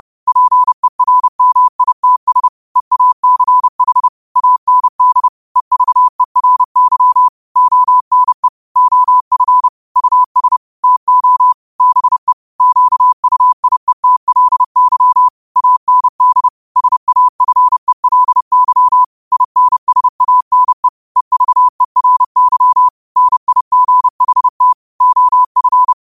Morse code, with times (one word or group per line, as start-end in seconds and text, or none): 0.3-2.5 PERMITS
2.8-4.1 EACH
4.4-5.3 AND
5.6-7.3 EVERY
7.6-8.5 ONE
8.8-9.7 OF
10.0-10.6 US
10.8-11.5 TO
11.8-12.3 BE
12.6-15.3 QUIETLY
15.6-16.5 AND
16.8-19.0 SAFELY
19.3-20.9 INSANE
21.2-22.9 EVERY
23.2-24.7 NIGHT
25.0-25.9 OF